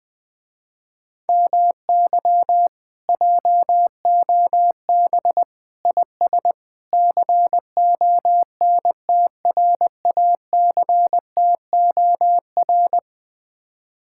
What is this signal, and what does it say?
Morse code, heard as a keyed tone, about 20 wpm, 700 Hz. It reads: MY JOB IS CONTRACTOR
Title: Speed-Racing Common QSO Phrases - 17wpm